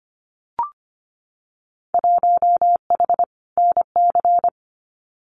Morse code, heard as a keyed tone, about 25 wpm, 700 Hz. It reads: 1H NC
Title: Winter Field Day Exchanges 25wpm